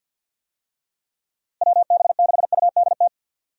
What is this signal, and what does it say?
Morse code, heard as a keyed tone, about 51 wpm, 700 Hz. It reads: WB6UDT
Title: USA Call Signs - 6 Characters - Repeated 3x - 50wpm